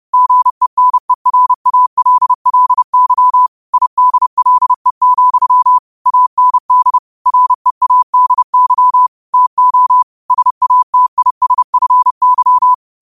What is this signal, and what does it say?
Morse code, heard as a keyed tone, about 30 wpm, 1000 Hz. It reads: GENERALLY IDLE, AND READY TO SATISFY
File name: qotdfiles / 30wpm